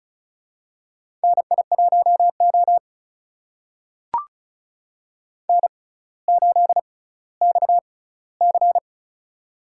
Morse code, heard as a keyed tone, about 36 wpm, spaced as for 18 wpm, 690 Hz. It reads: NI1O N8XC
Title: USA Call Signs - 4 Characters - 6x Character Spacing - 35wpm